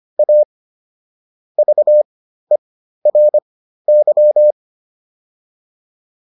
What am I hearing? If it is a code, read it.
Morse code, heard as a keyed tone, about 25 wpm, spaced as for 13 wpm, 600 Hz. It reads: A VERY